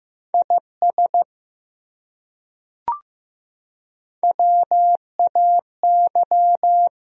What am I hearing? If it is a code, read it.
Morse code, heard as a keyed tone, about 15 wpm, 700 Hz. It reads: IS WAY
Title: Top 100 Words Repeated 15wpm